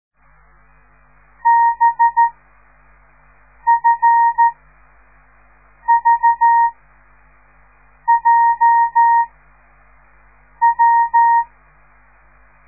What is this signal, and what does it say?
Morse code, heard as a keyed tone, about 13 wpm, 930 Hz. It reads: B F V J W